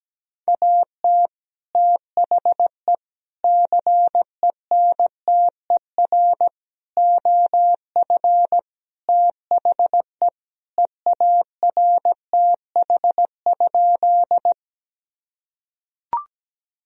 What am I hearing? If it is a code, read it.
Morse code, heard as a keyed tone, about 17 wpm, 700 Hz. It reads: AT THE CENTER OF THE EARTH?
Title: Sentences from Top 1000 Words - Review Part 2 - 17wpm